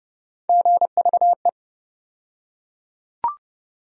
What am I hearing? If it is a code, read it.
Morse code, heard as a keyed tone, about 30 wpm, 700 Hz. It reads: GVE
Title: USA Call Sign Postfixes - 3 Letters 30wpm